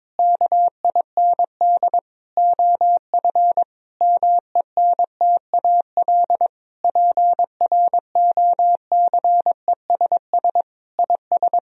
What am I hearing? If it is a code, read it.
Morse code, heard as a keyed tone, about 22 wpm, 700 Hz. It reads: KIND OF MENTAL PROCESS IS
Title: Sentences from Top 1000 Words - Review Part 4 - 22wpm